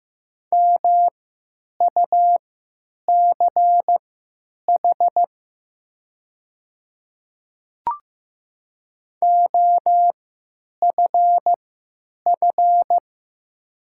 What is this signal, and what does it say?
Morse code, heard as a keyed tone, about 15 wpm, spaced as for 9 wpm, 700 Hz. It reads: MUCH OFF
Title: 3x Character Spacing - Top 300 Words - 15wpm